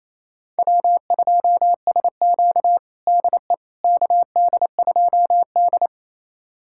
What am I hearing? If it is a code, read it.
Morse code, heard as a keyed tone, about 28 wpm, 700 Hz. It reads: W2SQ DE KD2D
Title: USA Callsign DE Callsign - 4 Characters - 28wpm